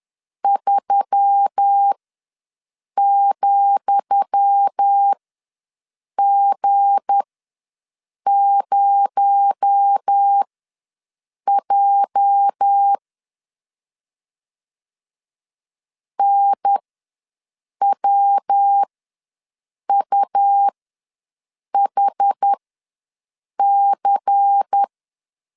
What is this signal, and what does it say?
Morse code, heard as a keyed tone, about 10 wpm, spaced as for 6 wpm, 800 Hz. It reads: 3,G0J NWUHC
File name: LEZIONI / 10 SIGNS / 10D COMA ws 10wpm 4314